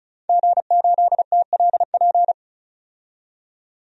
Morse code, heard as a keyed tone, about 34 wpm, 700 Hz. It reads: G8TLP
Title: DXCC-100 Call Signs - 5 Characters - 35wpm